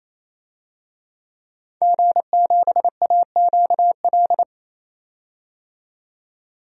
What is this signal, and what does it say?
Morse code, heard as a keyed tone, about 28 wpm, 700 Hz. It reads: G7AQL
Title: DXCC-100 Call Signs - 5 Characters - 28wpm